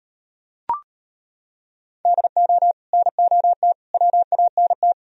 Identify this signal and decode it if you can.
Morse code, heard as a keyed tone, about 38 wpm, 700 Hz. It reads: DO NOT WANT